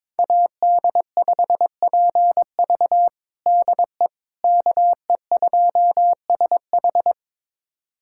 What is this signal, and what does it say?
Morse code, heard as a keyed tone, about 22 wpm, 700 Hz. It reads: AD5PV DE KE2SH
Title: USA Callsign DE Callsign - 5 Characters - 22wpm